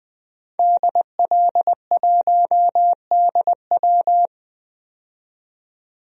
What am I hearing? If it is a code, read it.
Morse code, heard as a keyed tone, about 20 wpm, 700 Hz. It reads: DL1DW
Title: DXCC-100 Call Signs - 5 Characters - 20wpm